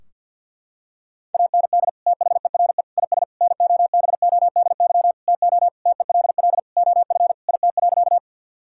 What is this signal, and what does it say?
Morse code, heard as a keyed tone, about 50 wpm, 700 Hz. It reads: AND THERE IS NOBODY TO TELL OF IT.
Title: Sentences from Top 1600 Words 50wpm